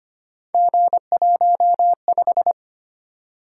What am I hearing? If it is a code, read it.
Morse code, heard as a keyed tone, about 25 wpm, 700 Hz. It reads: G15
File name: Three Letter-Number Combinations - 25wpm